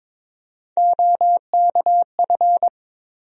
Morse code, heard as a keyed tone, about 22 wpm, 700 Hz. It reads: OKF
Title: USA Call Sign Postfixes - 3 Letters 22wpm